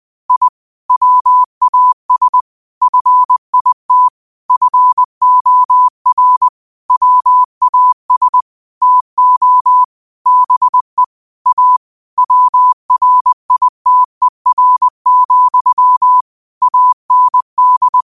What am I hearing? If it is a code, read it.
Morse code, heard as a keyed tone, about 20 wpm, 1000 Hz. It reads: I WAS FIT FOR WAS TO BE A WRITER, AND
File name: qotdfiles / 20wpm